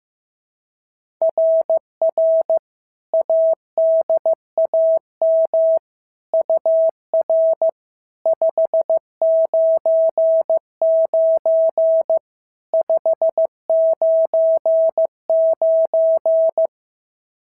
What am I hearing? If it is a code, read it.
Morse code, heard as a keyed tone, about 15 wpm, 650 Hz. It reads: RR ADAM UR 599 599
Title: SOTA QSO Practice 15wpm